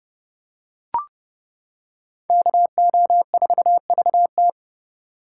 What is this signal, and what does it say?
Morse code, heard as a keyed tone, about 29 wpm, 700 Hz. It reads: KO4VT